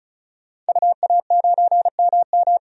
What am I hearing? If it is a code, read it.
Morse code, heard as a keyed tone, about 34 wpm, 700 Hz. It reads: UA9MM
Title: DXCC-100 Call Signs - 5 Characters - 35wpm